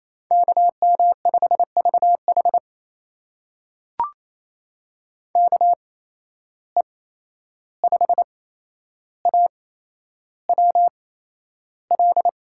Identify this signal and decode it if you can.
Morse code, heard as a keyed tone, about 28 wpm, spaced as for 11 wpm, 700 Hz. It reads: KM5VH KE5AWL
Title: USA Call Signs - 8x Character Spacing - 28wpm